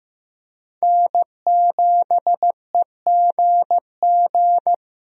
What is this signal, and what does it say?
Morse code, heard as a keyed tone, about 15 wpm, 700 Hz. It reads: N7EGG
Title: USA Call Signs - 5 Characters - Repeated 3x - 15wpm